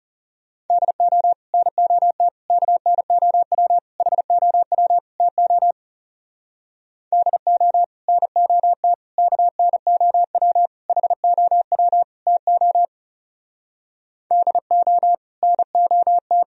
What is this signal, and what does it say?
Morse code, heard as a keyed tone, about 36 wpm, 700 Hz. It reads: DO NOT KNOW HOW TO DO NOT KNOW HOW TO DO NOT
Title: Speed-Racing - Sets of 5 Words 30wpm